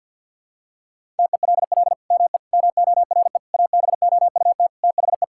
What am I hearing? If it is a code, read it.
Morse code, heard as a keyed tone, about 47 wpm, 700 Hz. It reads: TELL ME MORE ABOUT THE
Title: Sentences from Top 1000 Words - Review Part 5 - 50wpm